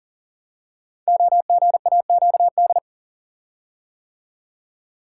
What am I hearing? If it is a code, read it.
Morse code, heard as a keyed tone, about 40 wpm, 700 Hz. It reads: OGAQD
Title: Code Groups - 5 Letters - 40wpm